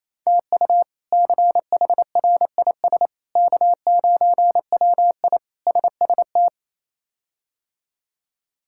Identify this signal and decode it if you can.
Morse code, heard as a keyed tone, about 28 wpm, 700 Hz. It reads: TU CHRIS K9WI SST
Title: K1USN SST QSO Practice - 28wpm